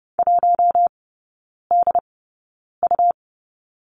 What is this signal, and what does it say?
Morse code, heard as a keyed tone, about 30 wpm, spaced as for 9 wpm, 700 Hz. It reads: 1DU
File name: Lesson 17e - Letter u - Three Characters 30wpm